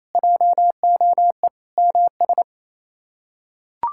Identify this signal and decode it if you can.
Morse code, heard as a keyed tone, about 28 wpm, 700 Hz. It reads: JOE MS
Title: K1USN SST QSO Practice - 28wpm